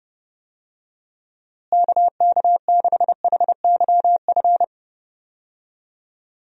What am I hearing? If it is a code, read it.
Morse code, heard as a keyed tone, about 30 wpm, 700 Hz. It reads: KK6HYF
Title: USA Call Signs - 6 Characters - Repeated 3x - 30wpm